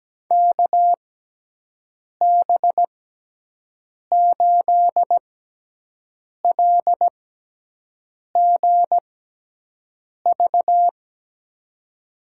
Morse code, heard as a keyed tone, about 17 wpm, spaced as for 6 wpm, 700 Hz. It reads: KB8LGV
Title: USA Call Signs - 6x Character Spacing - 17wpm